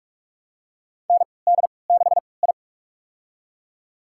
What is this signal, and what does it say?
Morse code, heard as a keyed tone, about 44 wpm, spaced as for 25 wpm, 700 Hz. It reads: ND6I
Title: USA Call Signs - 4 Characters - 3x Character Spacing - 45wpm